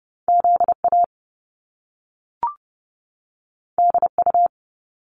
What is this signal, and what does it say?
Morse code, heard as a keyed tone, about 30 wpm, 700 Hz. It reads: ZA DU
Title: Top Two Letter Combinations in Words 30wpm